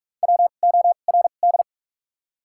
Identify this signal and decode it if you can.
Morse code, heard as a keyed tone, about 46 wpm, spaced as for 32 wpm, 700 Hz. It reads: WORD
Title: 2x Character Spacing - Top 200 Words - 45wpm